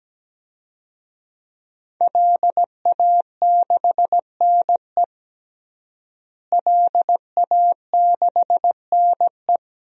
Morse code, heard as a keyed tone, about 17 wpm, 700 Hz. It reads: LA6NE LA6NE
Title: DXCC-100 Call Signs - 5 Characters - Repeated 3x - 17wpm